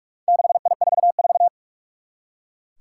Morse code, heard as a keyed tone, about 46 wpm, 700 Hz. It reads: 6I44
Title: Four Challenging Letter-Number Combinations 45wpm